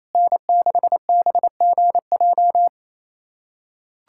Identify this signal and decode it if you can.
Morse code, heard as a keyed tone, about 28 wpm, 700 Hz. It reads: N6BGJ
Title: USA Callsign DE Callsign - 5 Characters - 28wpm